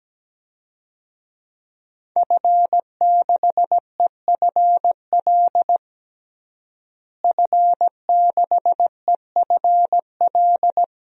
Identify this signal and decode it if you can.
Morse code, heard as a keyed tone, about 17 wpm, 700 Hz. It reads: F6EFL F6EFL
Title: DXCC-100 Call Signs - 5 Characters - Repeated 2x - 17wpm